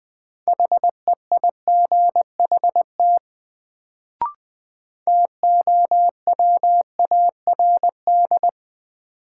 Morse code, heard as a keyed tone, about 20 wpm, 700 Hz. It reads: HEIGHT TOWARD
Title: Sets of 5 Words - New Word Drill - 20wpm